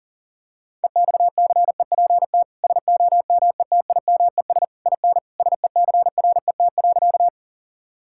Morse code, heard as a keyed tone, about 40 wpm, 700 Hz. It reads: EXCEPT SOMETIMES IN SECRET.
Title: Sentences from Top 1600 Words 40wpm